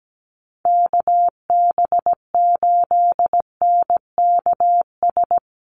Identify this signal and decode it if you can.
Morse code, heard as a keyed tone, about 17 wpm, 700 Hz. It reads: KB8NKS
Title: USA Call Signs Repeated 3x 17wpm